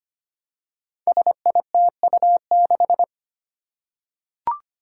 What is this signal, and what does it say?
Morse code, heard as a keyed tone, about 25 wpm, 700 Hz. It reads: SITU6
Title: Five Challenging Letter-Number Combinations 25wpm